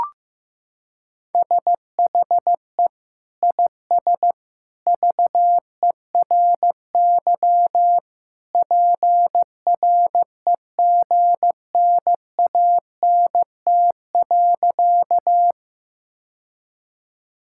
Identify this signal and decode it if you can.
Morse code, heard as a keyed tone, about 15 wpm, 700 Hz. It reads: SHE IS VERY PREGNANT.
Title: Sentences from Top 1000 Words - Review Part 6 - 15wpm